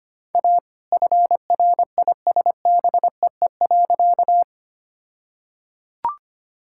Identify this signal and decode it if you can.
Morse code, heard as a keyed tone, about 25 wpm, 700 Hz. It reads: A FRISBEE.
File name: practice / 25wpm / All-Age Appropriate Jokes - 25wpm